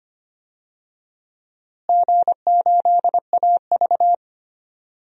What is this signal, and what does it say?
Morse code, heard as a keyed tone, about 25 wpm, 700 Hz. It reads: G8AV
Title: DXCC-100 Call Signs - 4 Characters - Repeated 3x - 25wpm